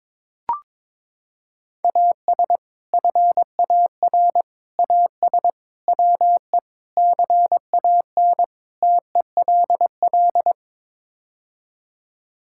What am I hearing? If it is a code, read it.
Morse code, heard as a keyed tone, about 23 wpm, 700 Hz. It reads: AS FAR AS WE CAN TELL
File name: practice / 22wpm / Sets of 6 Words - 22wpm